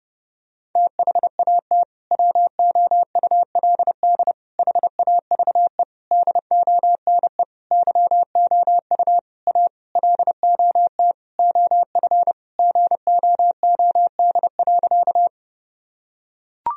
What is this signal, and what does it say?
Morse code, heard as a keyed tone, about 32 wpm, 700 Hz. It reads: THAT WOULD HAVE DONE YOU A LOT OF GOOD.
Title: Sentences from Top 1000 Words - Review Part 3 - 30wpm